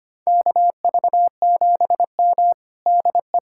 Morse code, heard as a keyed tone, about 25 wpm, 700 Hz. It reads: KV7M DE
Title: USA Callsign DE Callsign - 4 Characters - 25wpm